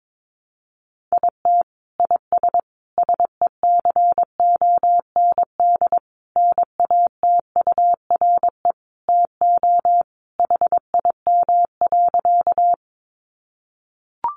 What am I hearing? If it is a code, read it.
Morse code, heard as a keyed tone, about 22 wpm, 700 Hz. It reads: IT IS SECOND NATURE TO HIM.
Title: Sentences from Top 600 Words 22wpm